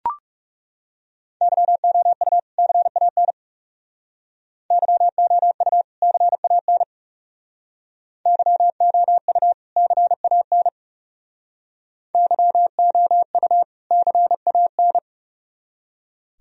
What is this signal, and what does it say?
Morse code, heard as a keyed tone, about 39 wpm, 700 Hz. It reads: YOU CAN YOU CAN YOU CAN YOU CAN